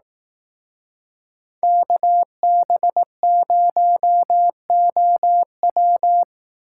Morse code, heard as a keyed tone, about 18 wpm, 700 Hz. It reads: KB0OW
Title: Speed Racing - USA 2x2 Call Signs 15wpm